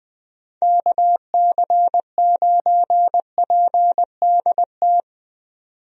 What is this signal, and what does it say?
Morse code, heard as a keyed tone, about 21 wpm, 700 Hz. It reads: KC9PDT
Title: USA Call Signs - 6 Characters - 20wpm